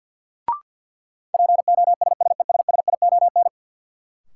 Morse code, heard as a keyed tone, about 48 wpm, 700 Hz. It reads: POSSESSION
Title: Sentences from Top 1400 Words - New Word Drill - 50wpm